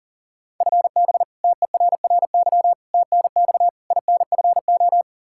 Morse code, heard as a keyed tone, about 40 wpm, 700 Hz. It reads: FB TERRY TNX INFO
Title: Practice QSO - Exchange 3 - More Info - 40wpm